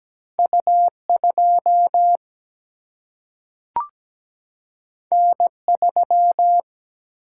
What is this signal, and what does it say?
Morse code, heard as a keyed tone, about 17 wpm, 700 Hz. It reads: U 2 N 3
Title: Single-Letter Single-Number 17wpm